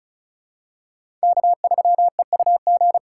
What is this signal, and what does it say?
Morse code, heard as a keyed tone, about 36 wpm, 700 Hz. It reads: K3EUG
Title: USA Call Signs - 4x Character Spacing - 35wpm